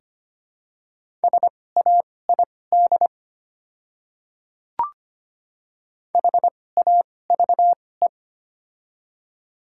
Morse code, heard as a keyed tone, about 25 wpm, spaced as for 18 wpm, 700 Hz. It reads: SAID HAVE
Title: 2x Character Spacing - Top 300 Words - 25wpm